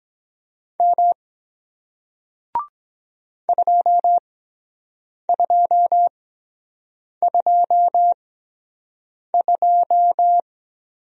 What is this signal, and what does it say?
Morse code, heard as a keyed tone, about 23 wpm, spaced as for 7 wpm, 700 Hz. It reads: M 2222